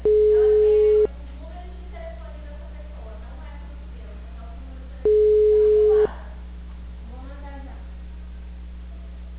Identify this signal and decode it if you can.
Ringback tone (Europe, 425 Hz), with speech.